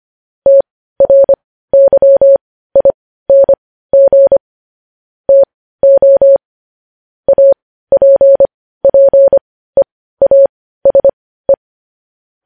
Morse code, heard as a keyed tone, about 25 wpm, spaced as for 15 wpm, 550 Hz. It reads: TRYING TO APPEASE
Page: News Headlines in Morse Code at 15 WPM